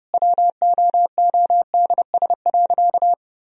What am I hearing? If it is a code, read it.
Morse code, heard as a keyed tone, about 30 wpm, 700 Hz. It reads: WOODS.
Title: Sentences from Top 1000 Words - Review Part 2 - 30wpm